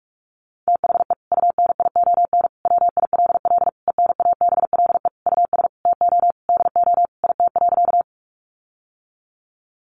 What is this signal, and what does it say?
Morse code, heard as a keyed tone, about 44 wpm, 700 Hz. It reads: THE UNION WILL ENABLE US TO DO IT.
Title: Sentences from Top 1500 Words 45wpm